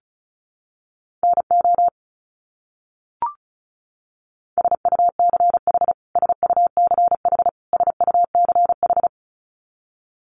Morse code, heard as a keyed tone, about 34 wpm, 700 Hz. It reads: NO SUCH SUCH SUCH
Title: Top 200 Words Repeated 35wpm